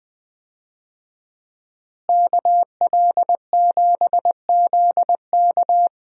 Morse code, heard as a keyed tone, about 20 wpm, 700 Hz. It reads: KL7ZK
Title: USA Call Signs - 5 Characters - Repeated 3x - 20wpm